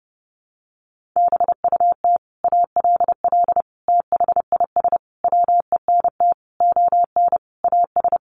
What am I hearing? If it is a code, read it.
Morse code, heard as a keyed tone, about 30 wpm, 700 Hz. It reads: BUT ALL THIS WENT ON AS
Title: Sentences from Top 1500 Words 30wpm